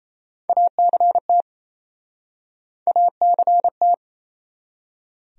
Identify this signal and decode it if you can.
Morse code, heard as a keyed tone, about 37 wpm, 700 Hz. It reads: ACT ACT